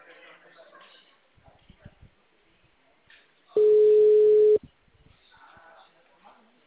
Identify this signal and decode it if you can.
Ringback tone (Europe, 425 Hz), with speech.